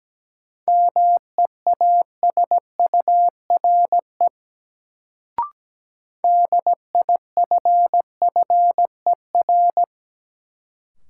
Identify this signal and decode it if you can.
Morse code, heard as a keyed tone, about 17 wpm, 700 Hz. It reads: MEASURE DIFFER
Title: Sentences from Top 1000 Words - Review Part 1 - New Word Drill - 17wpm